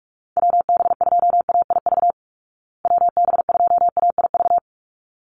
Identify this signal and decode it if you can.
Morse code, heard as a keyed tone, about 44 wpm, 700 Hz. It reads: WB2AIV WB2AIV
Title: Call Signs Repeated 3x - 45wpm